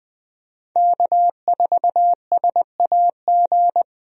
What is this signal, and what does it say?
Morse code, heard as a keyed tone, about 20 wpm, 700 Hz. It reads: K4SAG
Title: USA Call Signs - 5 Characters - 20wpm